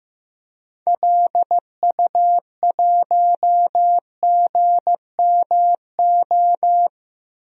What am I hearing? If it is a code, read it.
Morse code, heard as a keyed tone, about 15 wpm, 700 Hz. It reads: LU1GMO
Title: DXCC-100 Call Signs - 6 Characters - Repeated 2x - 15wpm